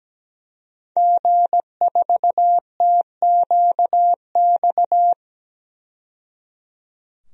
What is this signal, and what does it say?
Morse code, heard as a keyed tone, about 17 wpm, 700 Hz. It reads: G4TQX
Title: DXCC-100 Call Signs - 5 Characters - 17wpm